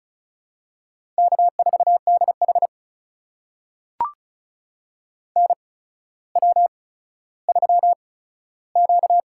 Morse code, heard as a keyed tone, about 36 wpm, spaced as for 15 wpm, 700 Hz. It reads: K4DH NW3Q
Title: USA Call Signs - 4 Characters - 8x Character Spacing - 35wpm